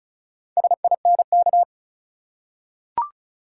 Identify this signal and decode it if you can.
Morse code, heard as a keyed tone, about 34 wpm, 700 Hz. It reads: SINK
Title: Four Letter Words - 35wpm